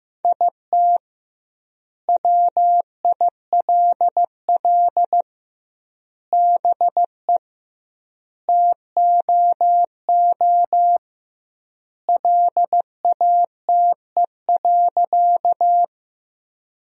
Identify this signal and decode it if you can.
Morse code, heard as a keyed tone, about 16 wpm, 700 Hz. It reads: IT WILL BE TOO LATE.
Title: 2x Word-Spacing - Sentences from Top 300 Words - 15wpm